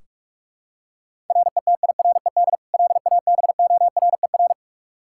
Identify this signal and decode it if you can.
Morse code, heard as a keyed tone, about 44 wpm, 700 Hz. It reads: RETIRED LABORER
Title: Practice QSO - Exchange 3 - More Info - 45wpm